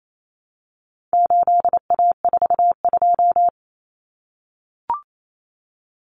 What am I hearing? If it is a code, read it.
Morse code, heard as a keyed tone, about 28 wpm, 700 Hz. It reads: 8A42
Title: Common Cut Numbers - 4-Digit 28wpm